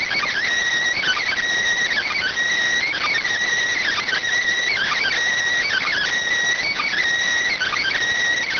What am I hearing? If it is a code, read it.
Emergency Alert System (SAME) tones (header bursts), heard as harsh buzzing bursts. It is heard from 1.2 to 7.6 s.